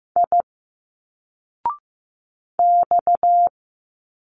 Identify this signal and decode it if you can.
Morse code, heard as a keyed tone, about 15 wpm, 700 Hz. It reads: I X
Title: Single Letters 15wpm